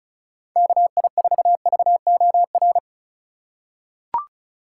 Morse code, heard as a keyed tone, about 34 wpm, 700 Hz. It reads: KI4VOR
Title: USA Call Signs - 6 Characters - 2x Character Spacing - 35wpm